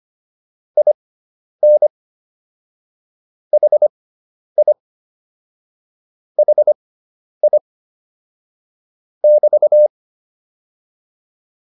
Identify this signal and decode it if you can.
Morse code, heard as a keyed tone, about 25 wpm, spaced as for 10 wpm, 600 Hz. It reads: IN HI HI <BT>
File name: QSO104 10